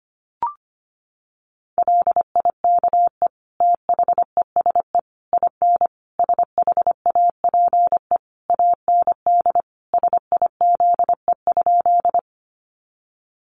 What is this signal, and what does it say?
Morse code, heard as a keyed tone, about 25 wpm, 700 Hz. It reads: LIKE THESE IN SHAPE AND SIZE?
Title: Sentences from Top 500 Words 25wpm